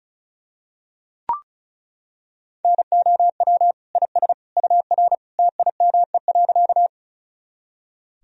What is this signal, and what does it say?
Morse code, heard as a keyed tone, about 34 wpm, 700 Hz. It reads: NOW IS UR TIME.
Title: Sentences from Top 100 Words and Abbreviations 35wpm